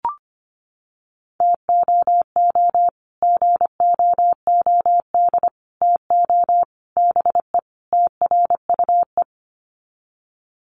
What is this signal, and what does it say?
Morse code, heard as a keyed tone, about 25 wpm, 700 Hz. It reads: TOO GOOD TO BE TRUE
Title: Sets of 5 Words Encore 25wpm